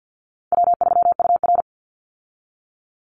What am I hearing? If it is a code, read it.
Morse code, heard as a keyed tone, about 50 wpm, 700 Hz. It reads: W3UR
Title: Easy Call Signs Repeated 2x - 50wpm